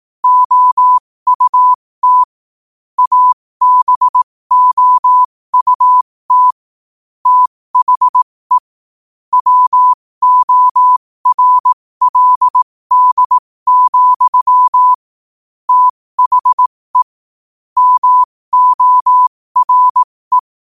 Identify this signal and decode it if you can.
Morse code, heard as a keyed tone, about 18 wpm, spaced as for 15 wpm, 1000 Hz. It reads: OUT ABOUT THE WORLD, THE MORE
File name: qotdfiles / 15wpm